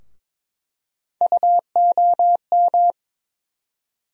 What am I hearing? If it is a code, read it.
Morse code, heard as a keyed tone, about 22 wpm, 700 Hz. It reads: UOM
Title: USA Call Sign Postfixes - 3 Letters 22wpm